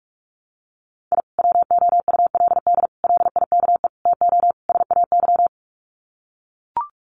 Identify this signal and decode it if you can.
Morse code, heard as a keyed tone, about 44 wpm, 700 Hz. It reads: I WOULD LIKE TO SAY
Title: Sets of 5 Words Encore2 45wpm